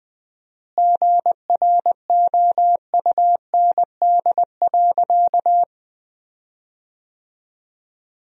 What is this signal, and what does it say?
Morse code, heard as a keyed tone, about 21 wpm, 700 Hz. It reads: GROUND.
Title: 4x Word-Spacing - Sentences from Top 500 Words - 20wpm